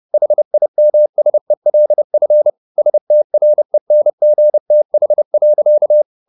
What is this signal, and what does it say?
Morse code, heard as a keyed tone, about 30 wpm, 600 Hz. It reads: HIMSELF STRENGTH.
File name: the old man and the sea / the old man and the sea 1 30wpm